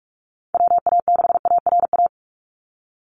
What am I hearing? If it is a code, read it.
Morse code, heard as a keyed tone, about 46 wpm, 690 Hz. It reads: WA6ARA
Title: NA SOTA Chaser Callsigns 45wpm